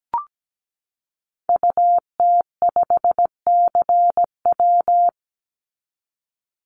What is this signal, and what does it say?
Morse code, heard as a keyed tone, about 17 wpm, 700 Hz. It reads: UT5CW
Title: Call Signs Repeated 2x - 17wpm